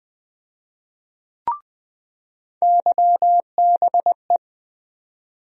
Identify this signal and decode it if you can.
Morse code, heard as a keyed tone, about 20 wpm, 700 Hz. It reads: YBE
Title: USA Call Sign Postfixes - 3 Letters 20wpm